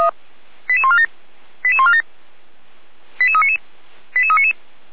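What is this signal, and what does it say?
Five-tone selective call (ZVEI), heard as quick runs of five tones, tones of about 70 ms. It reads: 88147 88147 88399 88399
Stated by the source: ZVEI five-tone call received by SDR, from feuerwehr-orlamuende.org fire brigade archive.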